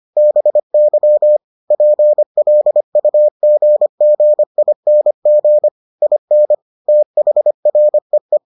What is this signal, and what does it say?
Morse code, heard as a keyed tone, about 25 wpm, 600 Hz. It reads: BY PLUGGING IN THREE